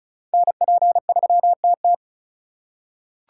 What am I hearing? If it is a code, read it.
Morse code, heard as a keyed tone, about 36 wpm, 700 Hz. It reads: NP3TT
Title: USA Call Signs - 4x Character Spacing - 35wpm